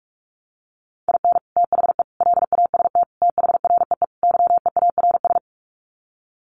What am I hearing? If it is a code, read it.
Morse code, heard as a keyed tone, about 44 wpm, 700 Hz. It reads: IN THE LAST THREE YEARS
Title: Sets of 5 Words Encore2 45wpm